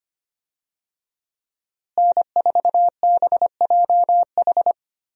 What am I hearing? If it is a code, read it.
Morse code, heard as a keyed tone, about 25 wpm, 700 Hz. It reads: N4BJH
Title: USA Call Signs - 5 Characters - Repeated 3x - 25wpm